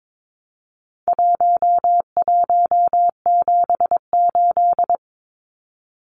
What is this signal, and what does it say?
Morse code, heard as a keyed tone, about 22 wpm, 700 Hz. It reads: 1178
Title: Numbers - 4-Digit 22wpm